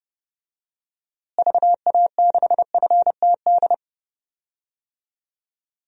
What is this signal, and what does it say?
Morse code, heard as a keyed tone, about 30 wpm, 700 Hz. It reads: VA6FTD
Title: DXCC-100 Call Signs - 6 Characters - Repeated 3x - 30wpm